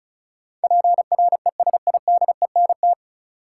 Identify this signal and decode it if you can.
Morse code, heard as a keyed tone, about 36 wpm, 700 Hz. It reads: PRESIDENT
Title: Sentences from Top 1000 Words - New Word Drill - 35wpm